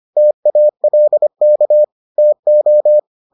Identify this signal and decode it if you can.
Morse code, heard as a keyed tone, about 25 wpm, 600 Hz. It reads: TALK TO